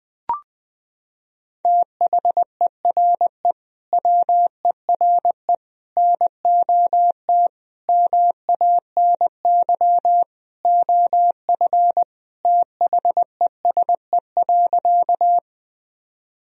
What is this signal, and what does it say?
Morse code, heard as a keyed tone, about 20 wpm, 700 Hz. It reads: THERE WERE NOT MANY OF THESE.
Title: Sentences from Top 1000 Words - Review Part 1 - 20wpm